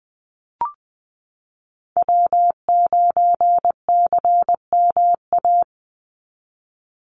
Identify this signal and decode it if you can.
Morse code, heard as a keyed tone, about 20 wpm, 700 Hz. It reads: W9CMA
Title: USA Call Signs 20wpm